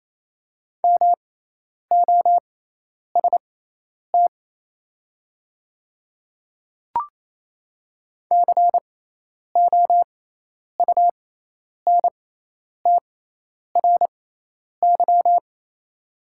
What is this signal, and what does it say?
Morse code, heard as a keyed tone, about 28 wpm, spaced as for 10 wpm, 700 Hz. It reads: MOST COUNTRY
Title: 6x Character Spacing - Top 300 Words - 28wpm